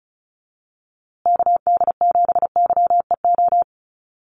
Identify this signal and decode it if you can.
Morse code, heard as a keyed tone, about 34 wpm, 700 Hz. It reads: KD7YEO
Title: USA Call Signs Repeated 2x 35wpm